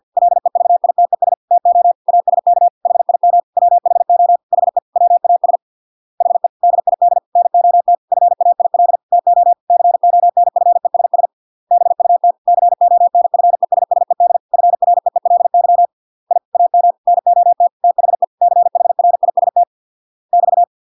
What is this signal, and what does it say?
Morse code, heard as a keyed tone, about 50 wpm, 700 Hz. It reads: LEVITES TO ASK HIM WHO HE WAS HE DID NOT FAIL TO CONFESS BUT CONFESSED FREELY I AM NOT THE CHRIST <BT>